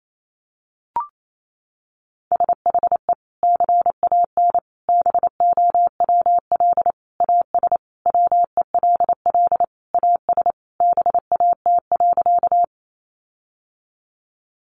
Morse code, heard as a keyed tone, about 28 wpm, 700 Hz. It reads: SHE CAN BOWL AS WELL AS BAT.
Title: Sentences from Top 1500 Words 28wpm